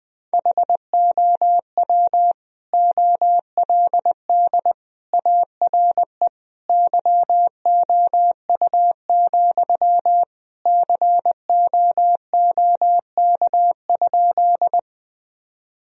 Morse code, heard as a keyed tone, about 21 wpm, 700 Hz. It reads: HOW OLD ARE YOU, COOK?
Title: Sentences from Top 1000 Words - Review Part 4 - 20wpm